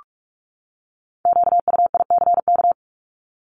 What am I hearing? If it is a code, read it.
Morse code, heard as a keyed tone, about 44 wpm, 700 Hz. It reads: QUICK
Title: Top 500 Words 45wpm